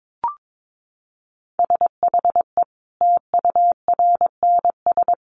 Morse code, heard as a keyed tone, about 22 wpm, 700 Hz. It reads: SHE TURNS
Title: Additional Easy Phrases 22wpm